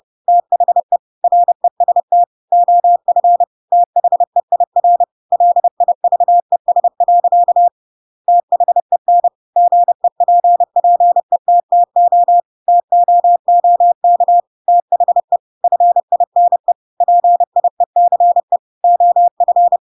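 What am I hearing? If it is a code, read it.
Morse code, heard as a keyed tone, about 30 wpm, 700 Hz. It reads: THE REST OF THEIR LIVES. THEN GEPPETTO TOOK THE FINE PIECE OF
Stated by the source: Chapter 2 of The Adventures of Pinocchio read as a book in Morse code (CW) at 30 wpm, by ik1hge.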